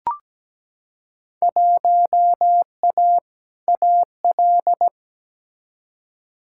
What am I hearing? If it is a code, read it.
Morse code, heard as a keyed tone, about 17 wpm, 700 Hz. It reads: 1A AL
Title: Field Day Exchanges - 17wpm